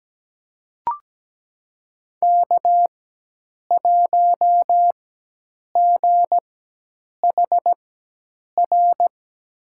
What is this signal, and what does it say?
Morse code, heard as a keyed tone, about 17 wpm, spaced as for 8 wpm, 700 Hz. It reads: K1GHR
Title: USA Call Signs - 5 Characters - 4x Character Spacing - 17wpm